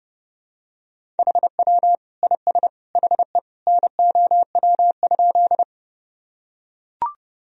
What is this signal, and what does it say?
Morse code, heard as a keyed tone, about 32 wpm, 700 Hz. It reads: HW IS HE NOW?
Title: Sentences from Top 100 Words and Abbreviations 30wpm